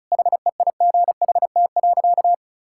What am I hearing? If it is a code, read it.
Morse code, heard as a keyed tone, about 36 wpm, 700 Hz. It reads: HEIGHT.
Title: Sentences from Top 1600 Words 35wpm